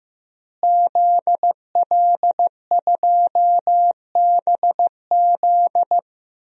Morse code, heard as a keyed tone, about 15 wpm, 700 Hz. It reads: ZL2BZ
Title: DXCC-100 Call Signs - 5 Characters - Repeated 3x - 15wpm